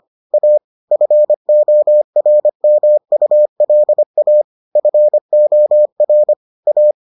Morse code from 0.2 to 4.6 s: A FORMULA

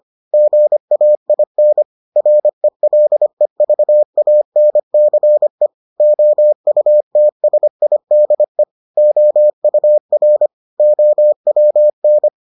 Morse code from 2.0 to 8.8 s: RELEVANCE OUTSIDE

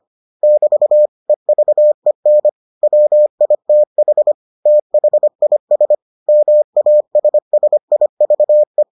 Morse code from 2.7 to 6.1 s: WITH THIS

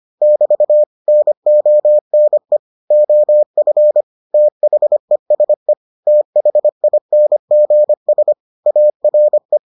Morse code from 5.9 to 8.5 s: THINGS

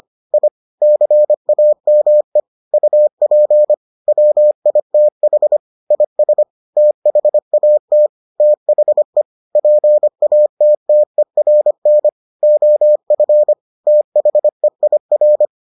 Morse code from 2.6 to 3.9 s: UP

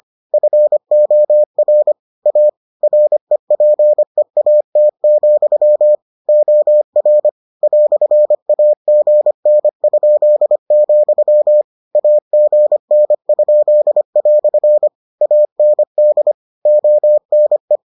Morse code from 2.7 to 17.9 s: REPEAT, OR "AGN?, AGN?" AND ONE

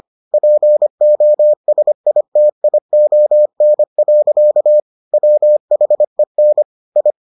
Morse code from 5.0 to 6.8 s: WHEN

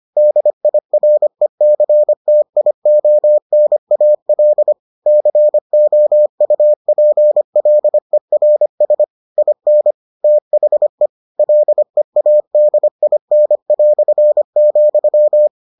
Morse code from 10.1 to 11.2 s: THE

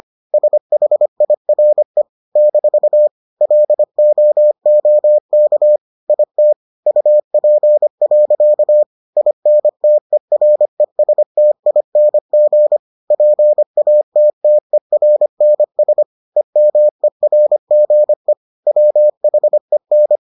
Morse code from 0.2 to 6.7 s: SHIRE - LOOK IT

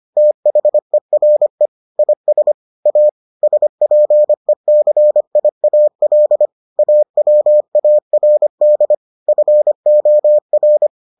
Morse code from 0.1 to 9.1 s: THERE IS A SPECIAL AWARD

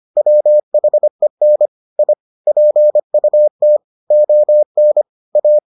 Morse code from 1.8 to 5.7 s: I PUT ON A

